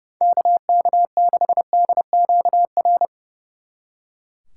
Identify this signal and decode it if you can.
Morse code, heard as a keyed tone, about 34 wpm, 700 Hz. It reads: KK6DQR
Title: Speed Racing - USA 2x3 Call Signs 30wpm